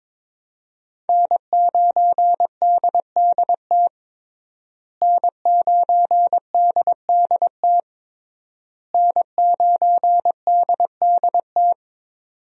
Morse code, heard as a keyed tone, about 22 wpm, 700 Hz. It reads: N9DDT N9DDT N9DDT
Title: USA Call Signs - 5 Characters - Repeated 3x - 22wpm